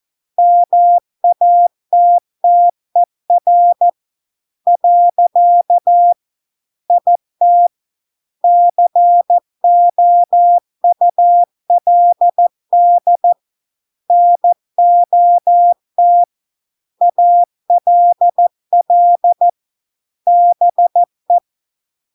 Morse code, heard as a keyed tone, about 14 wpm, 700 Hz. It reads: MATTER . IT COULD NOT ALL BE